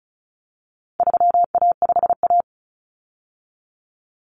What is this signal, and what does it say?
Morse code, heard as a keyed tone, about 34 wpm, 700 Hz. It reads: 3A5A
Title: Common Cut Numbers - 4-Digit 35wpm